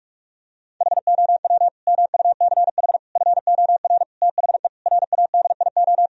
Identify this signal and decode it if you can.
Morse code, heard as a keyed tone, about 44 wpm, 700 Hz. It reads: HOW MUCH FOR THE RADIO
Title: Sentences from Top 1000 Words - Review Part 3 - 45wpm